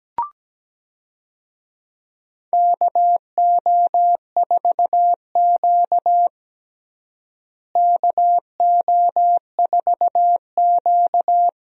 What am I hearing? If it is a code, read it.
Morse code, heard as a keyed tone, about 17 wpm, 700 Hz. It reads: KO4Q KO4Q
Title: USA Call Signs - 4 Characters - Repeated 3x - 17wpm